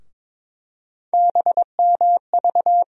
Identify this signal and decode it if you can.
Morse code, heard as a keyed tone, about 22 wpm, 700 Hz. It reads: BMV
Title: USA Call Sign Postfixes - 3 Letters 22wpm